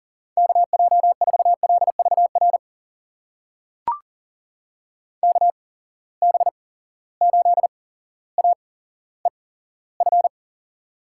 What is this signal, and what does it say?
Morse code, heard as a keyed tone, about 40 wpm, spaced as for 17 wpm, 700 Hz. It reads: KJ4LVR KB8AEF
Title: USA Call Signs - 6 Characters - 8x Character Spacing - 40wpm